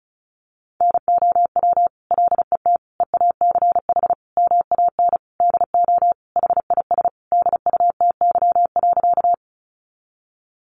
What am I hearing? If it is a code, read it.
Morse code, heard as a keyed tone, about 34 wpm, 700 Hz. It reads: NOW LET EACH MAN DO HIS DUTY.
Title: Sentences from Top 1500 Words 35wpm